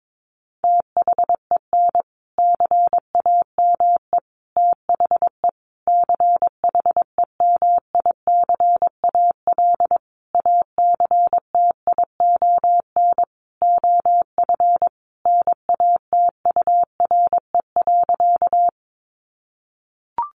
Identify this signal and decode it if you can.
Morse code, heard as a keyed tone, about 22 wpm, 700 Hz. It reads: THEN CAME THE CHEMICAL ACTION OF NATURE.
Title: Sentences from Top 1200 Words 22wpm